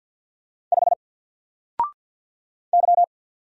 Morse code, heard as a keyed tone, about 47 wpm, 700 Hz. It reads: 5 Y
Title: Single Letter-Number Alt 50wpm